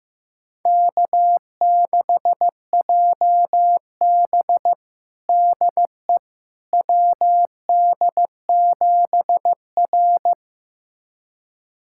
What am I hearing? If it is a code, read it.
Morse code, heard as a keyed tone, about 15 wpm, 700 Hz. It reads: K6JB DE WD7R
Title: USA Callsign DE Callsign - 4 Characters - 15wpm